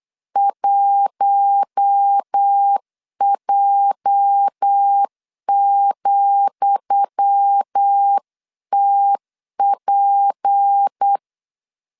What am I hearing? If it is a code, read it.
Morse code, heard as a keyed tone, about 8 wpm, 800 Hz. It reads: 1J,TP
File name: LEZIONI / 10 SIGNS / 10D COMA 8wpm 3749